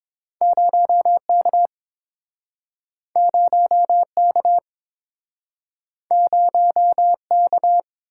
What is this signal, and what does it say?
Morse code, heard as a keyed tone, about 30 wpm, 700 Hz. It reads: 0K 0K 0K